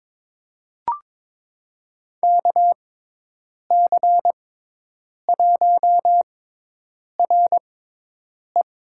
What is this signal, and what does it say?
Morse code, heard as a keyed tone, about 22 wpm, spaced as for 8 wpm, 700 Hz. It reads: KC1RE
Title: USA Call Signs - 6x Character Spacing - 22wpm